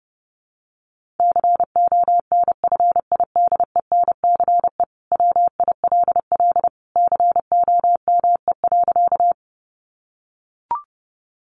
Morse code, heard as a keyed tone, about 28 wpm, 700 Hz. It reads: CONFIDENCE WILL COME.
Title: Sentences from Top 1500 Words 30wpm